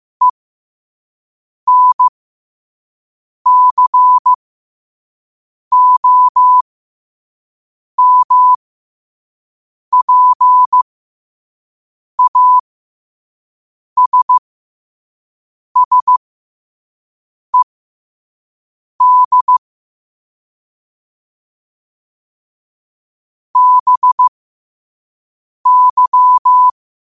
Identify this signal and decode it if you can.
Morse code, heard as a keyed tone, about 15 wpm, spaced as for 5 wpm, 1000 Hz. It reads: ENCOMPASSED BY
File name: qotdfiles / 5wpm